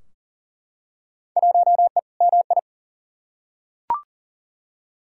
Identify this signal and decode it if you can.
Morse code, heard as a keyed tone, about 40 wpm, 700 Hz. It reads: 1E MI
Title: Field Day Exchanges - 40wpm